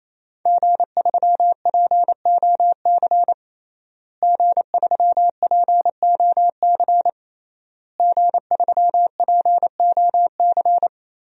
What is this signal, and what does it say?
Morse code, heard as a keyed tone, about 28 wpm, 700 Hz. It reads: G3POC G3POC G3POC
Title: DXCC-100 Call Signs - 5 Characters - Repeated 3x - 28wpm